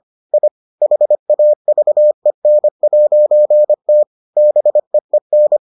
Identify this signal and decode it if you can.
Morse code, heard as a keyed tone, about 25 wpm, 600 Hz. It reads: I HAVEN'T BEEN